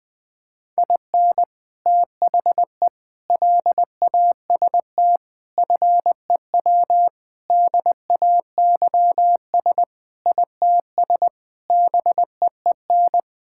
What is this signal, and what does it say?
Morse code, heard as a keyed tone, about 20 wpm, 700 Hz. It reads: IN THE LAST FEW DAYS ITS BEEN
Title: Sentences with 3-Word Phrases - Spoken Beginning - 20wpm